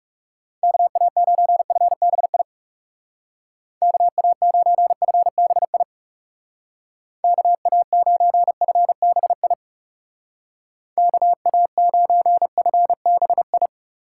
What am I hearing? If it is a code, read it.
Morse code, heard as a keyed tone, about 36 wpm, 700 Hz. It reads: KA9FBI KA9FBI KA9FBI KA9FBI
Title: Speed Racing - USA 2x3 Call Signs 30wpm